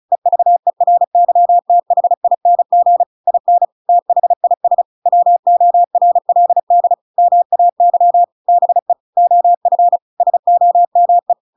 Morse code, heard as a keyed tone, about 35 wpm, 700 Hz. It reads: EVERYTHING IN THIS WORLD MAY BE OF SOME
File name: Pinocchio / Pinocchio ch7 35 35 ik1hge